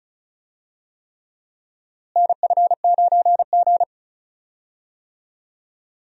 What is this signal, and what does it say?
Morse code, heard as a keyed tone, about 34 wpm, 700 Hz. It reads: NF9G
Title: USA Call Signs - 4 Characters - Repeated 3x - 35wpm